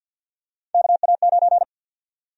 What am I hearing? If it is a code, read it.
Morse code, heard as a keyed tone, about 53 wpm, 700 Hz. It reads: KA9
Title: USA Call Sign Prefixes - 50wpm